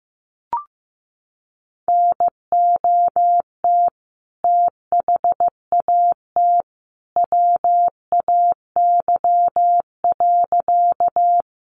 Morse code, heard as a keyed tone, about 15 wpm, 700 Hz. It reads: NOT THAT WAY.
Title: A Sound Of Thunder 15wpm